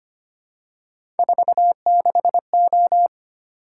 Morse code, heard as a keyed tone, about 25 wpm, 700 Hz. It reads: 46O
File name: Three Letter-Number Combinations - 25wpm